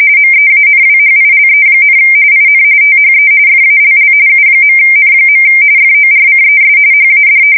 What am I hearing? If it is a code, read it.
RTTY, about 45.45 baud. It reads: MEANWHILE, IN THE FALL U.N. FORCES HAD
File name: fsk170 / battleshipnj